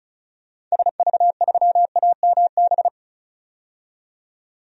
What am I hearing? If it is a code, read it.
Morse code, heard as a keyed tone, about 34 wpm, 700 Hz. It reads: SV3AMB
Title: DXCC-100 Call Signs - 6 Characters - 35wpm